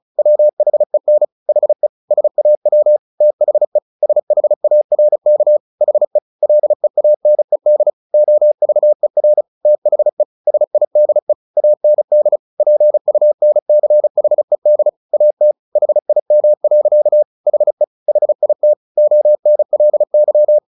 Morse code, heard as a keyed tone, about 34 wpm, 600 Hz. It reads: WHEN HE SAW THE SHARK HE LEANED OVER THE SIDE AND PUNCHED AT HIM. HE HIT ONLY
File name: the old man and the sea 2 35wpm